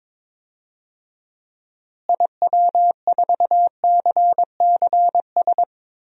Morse code, heard as a keyed tone, about 21 wpm, 700 Hz. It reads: IW4CCS
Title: DXCC-100 Call Signs - 6 Characters - Repeated 3x - 22wpm